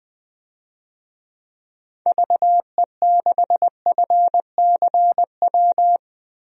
Morse code, heard as a keyed tone, about 20 wpm, 700 Hz. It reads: VE6FCW
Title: DXCC-100 Call Signs - 6 Characters - 20wpm